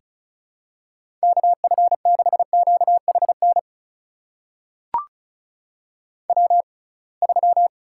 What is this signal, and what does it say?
Morse code, heard as a keyed tone, about 36 wpm, spaced as for 27 wpm, 700 Hz. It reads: KF6QHN W3
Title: USA Call Signs - 6x Character Spacing - 35wpm